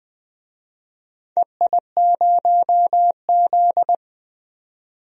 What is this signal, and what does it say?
Morse code, heard as a keyed tone, about 20 wpm, 700 Hz. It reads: EI0Z
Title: DXCC-100 Call Signs - 4 Characters - 20wpm